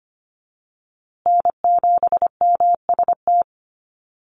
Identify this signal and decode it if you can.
Morse code, heard as a keyed tone, about 25 wpm, 700 Hz. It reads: N7MST
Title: USA Call Signs Repeated 2x 25wpm